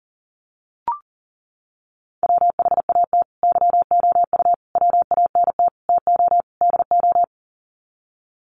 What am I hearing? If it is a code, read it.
Morse code, heard as a keyed tone, about 40 wpm, 700 Hz. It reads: WHAT YOU WANT TO DO
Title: Sets of 5 Words Encore 40wpm